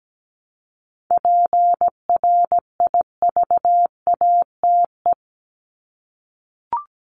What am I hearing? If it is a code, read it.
Morse code, heard as a keyed tone, about 17 wpm, 700 Hz. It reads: PRIVATE
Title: Top 751-1000 Words 17wpm